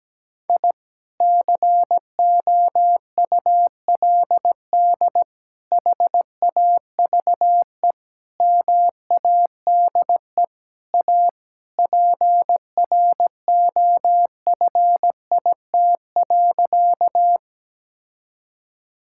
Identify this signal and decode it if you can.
Morse code, heard as a keyed tone, about 17 wpm, 700 Hz. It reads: I COULD HAVE MADE A PROFIT.
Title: Sentences from Top 1000 Words - Review Part 5 - 17wpm